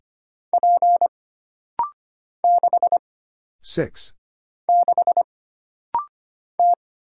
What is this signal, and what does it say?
Morse code, heard as a keyed tone, about 25 wpm, 700 Hz. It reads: P 6 6 T
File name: Single Character (Letter:Number:Punctuation:Prosign) - Rapid-Fire - 25wpm